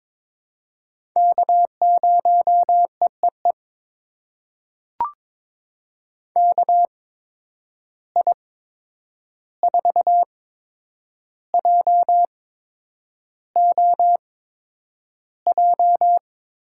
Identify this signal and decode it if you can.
Morse code, heard as a keyed tone, about 22 wpm, spaced as for 9 wpm, 700 Hz. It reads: K0EEE KI4JOJ
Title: USA Call Signs - 8x Character Spacing - 22wpm